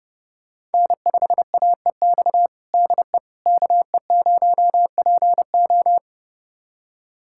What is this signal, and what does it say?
Morse code, heard as a keyed tone, about 30 wpm, 700 Hz. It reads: N5AEX DE KE0PO
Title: USA Callsign DE Callsign - 5 Characters - 30wpm